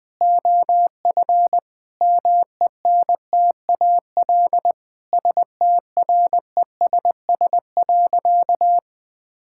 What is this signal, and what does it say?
Morse code, heard as a keyed tone, about 21 wpm, 700 Hz. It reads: OF MENTAL STRESS.
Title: Sentences from Top 1000 Words - Review Part 6 - 20wpm